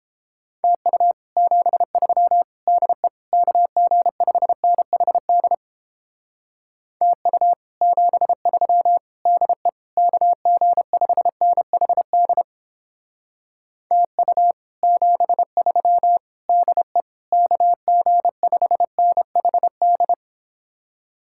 Morse code, heard as a keyed tone, about 29 wpm, 700 Hz. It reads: TU 73 DE KG5NHD TU 73 DE KG5NHD TU 73 DE KG5NHD